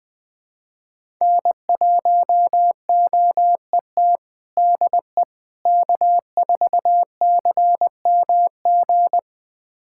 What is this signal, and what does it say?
Morse code, heard as a keyed tone, about 21 wpm, 700 Hz. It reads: N1OET DE K4CMG
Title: USA Callsign DE Callsign - 5 Characters - 20wpm